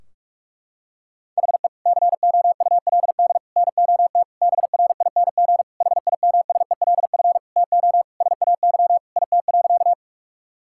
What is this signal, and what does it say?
Morse code, heard as a keyed tone, about 41 wpm, 700 Hz. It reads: HE COULD NOT BRING HIMSELF TO SAY IT.
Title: Sentences from Top 1600 Words 45wpm